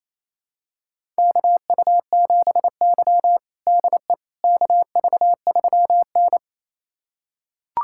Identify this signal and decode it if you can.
Morse code, heard as a keyed tone, about 28 wpm, 700 Hz. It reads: KU7Y DE KV3N
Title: USA Callsign DE Callsign - 4 Characters - 28wpm